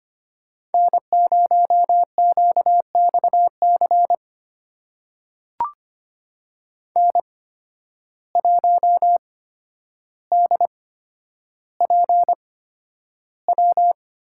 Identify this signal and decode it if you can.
Morse code, heard as a keyed tone, about 25 wpm, spaced as for 11 wpm, 700 Hz. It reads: N0QXC N1DPW
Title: USA Call Signs - 5 Characters - 8x Character Spacing - 25wpm